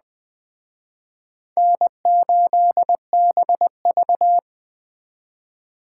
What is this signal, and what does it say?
Morse code, heard as a keyed tone, about 20 wpm, 700 Hz. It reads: N8BV
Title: Speed Racing - USA 1x2 Call Signs 17wpm